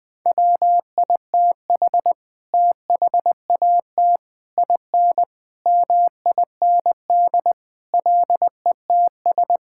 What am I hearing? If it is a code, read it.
Morse code, heard as a keyed tone, about 21 wpm, 700 Hz. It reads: WITH THAT IN MIND LETS
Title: Sets of 9 Words - 20wpm